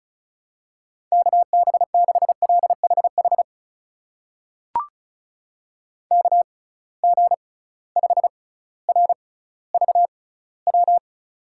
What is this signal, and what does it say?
Morse code, heard as a keyed tone, about 36 wpm, spaced as for 18 wpm, 700 Hz. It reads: KB6LHH KG5RVW
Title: USA Call Signs - 6 Characters - 6x Character Spacing - 35wpm